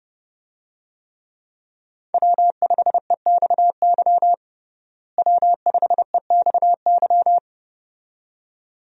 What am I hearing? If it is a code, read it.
Morse code, heard as a keyed tone, about 30 wpm, 700 Hz. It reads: W5EXY W5EXY
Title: USA Call Signs - 5 Characters - Repeated 2x - 30wpm